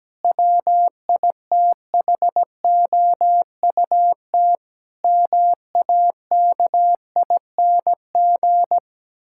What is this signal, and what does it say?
Morse code, heard as a keyed tone, about 17 wpm, 700 Hz. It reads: WITHOUT MAKING
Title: Sentences from Top 1000 Words - Review Part 6 - 17wpm